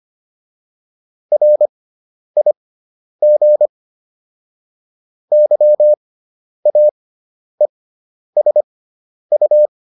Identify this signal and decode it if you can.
Morse code, heard as a keyed tone, about 25 wpm, spaced as for 10 wpm, 600 Hz. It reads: RIG YAESU
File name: QSO303 10